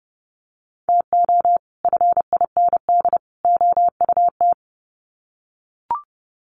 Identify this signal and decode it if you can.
Morse code, heard as a keyed tone, about 30 wpm, 700 Hz. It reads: TO FIND OUT
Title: Sets of 3 Words 30wpm